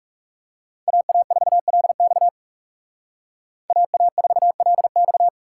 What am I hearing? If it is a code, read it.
Morse code, heard as a keyed tone, about 38 wpm, 700 Hz. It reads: AA4LX AA4LX